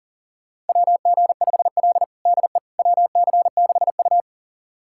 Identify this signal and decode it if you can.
Morse code, heard as a keyed tone, about 40 wpm, 700 Hz. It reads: WG5L DE WC6U
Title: USA Callsign DE Callsign - 4 Characters - 40wpm